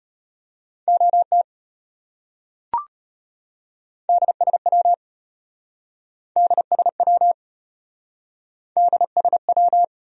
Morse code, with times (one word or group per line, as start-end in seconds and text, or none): 0.9-1.4 OT
4.1-4.9 DSW
6.4-7.3 DSW
8.8-9.8 DSW